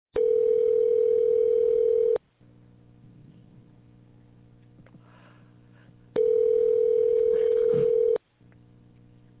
Ringback tone (North America), 440 + 480 Hz.